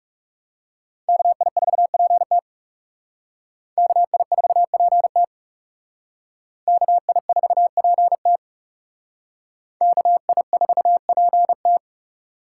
Morse code, heard as a keyed tone, about 37 wpm, 700 Hz. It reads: KI4PT KI4PT KI4PT KI4PT
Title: Speed Racing - USA 2x2 Call Signs 30wpm